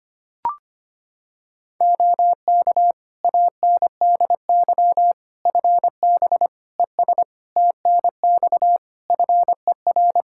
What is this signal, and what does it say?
Morse code, heard as a keyed tone, about 25 wpm, 700 Hz. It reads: OK ANDY FB ES TNX FER
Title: Practice QSO - Exchange 2 - Info - 25wpm